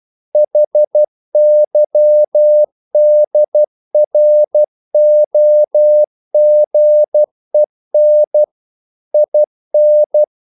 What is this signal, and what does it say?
Morse code, heard as a keyed tone, about 12 wpm, 600 Hz. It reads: HYDROGEN IN
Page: English quotations in CW (telegraphy - morse)